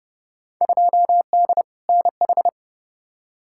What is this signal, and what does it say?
Morse code, heard as a keyed tone, about 30 wpm, 700 Hz. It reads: 2D NH
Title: Field Day Exchanges Encore - 30wpm